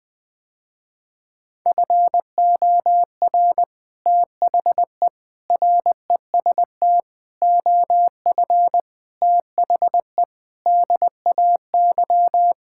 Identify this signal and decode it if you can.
Morse code, heard as a keyed tone, about 20 wpm, 700 Hz. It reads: FOR THE REST OF THE DAY
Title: Sentences with 3-Word Phrases - Spoken Beginning - 20wpm